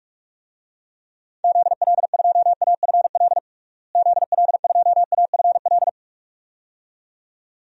Morse code, heard as a keyed tone, about 46 wpm, 700 Hz. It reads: ZL2AFL ZL2AFL
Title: DXCC-100 Call Signs - 6 Characters - Repeated 2x - 45wpm